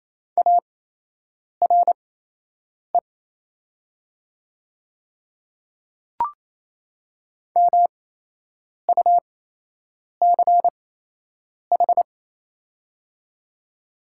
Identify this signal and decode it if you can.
Morse code, heard as a keyed tone, about 28 wpm, spaced as for 8 wpm, 700 Hz. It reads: ARE MUCH
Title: 8x Character Spacing - Top 200 Words - 28wpm